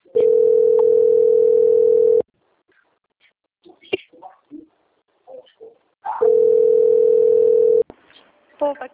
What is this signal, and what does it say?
Ringback tone (North America), 440 + 480 Hz, with speech.
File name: out- - - - - .18384